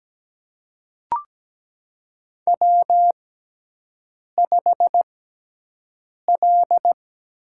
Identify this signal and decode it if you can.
Morse code, heard as a keyed tone, about 17 wpm, spaced as for 6 wpm, 700 Hz. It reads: W5L
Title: USA Call Signs - 4 Characters - 6x Character Spacing - 17wpm